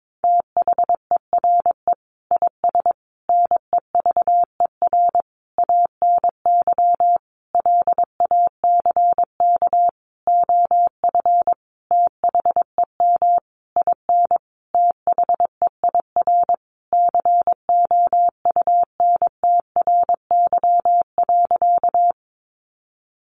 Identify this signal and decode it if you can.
Morse code, heard as a keyed tone, about 22 wpm, 700 Hz. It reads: THERE IS NEVER ANY LACK OF THEM IN THEIR COUNTRY.
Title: Sentences from Top 1100 Words 22wpm